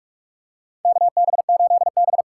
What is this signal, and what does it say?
Morse code, heard as a keyed tone, about 45 wpm, 700 Hz. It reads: KB8B
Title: USA Call Signs - 4 Characters - 3x Character Spacing - 45wpm